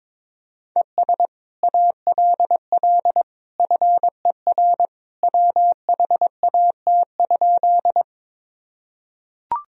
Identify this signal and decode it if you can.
Morse code, heard as a keyed tone, about 22 wpm, 700 Hz. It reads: ES ALL FER WHAT?
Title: Sentences from Top 100 Words and Abbreviations 22wpm